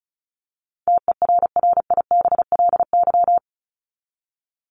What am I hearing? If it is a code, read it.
Morse code, heard as a keyed tone, about 34 wpm, 700 Hz. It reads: TERRIBLY
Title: Top 1751-2000 Words 35wpm